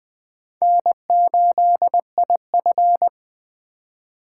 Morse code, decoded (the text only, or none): N8IF